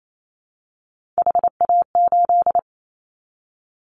Morse code, heard as a keyed tone, about 28 wpm, 700 Hz. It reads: HA8
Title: Callsign Prefixes for Top 50 DXCC - 28wpm